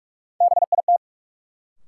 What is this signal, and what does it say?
Morse code, heard as a keyed tone, about 46 wpm, 700 Hz. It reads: BIT
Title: Sets of 3 Words - New Word Drill - 45wpm